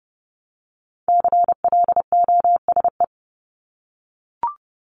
Morse code, heard as a keyed tone, about 30 wpm, 700 Hz. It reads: CLOSE
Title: LY Suffix - Top 100 Words with Base Words 30wpm